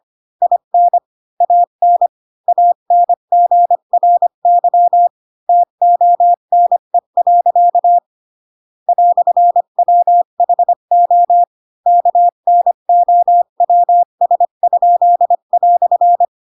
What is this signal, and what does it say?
Morse code, heard as a keyed tone, about 25 wpm, 700 Hz. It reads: IN AN ANGRY TONE. "WHO KNOWS?"
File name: Pinocchio / Pinocchio ch7 20 25 ik1hge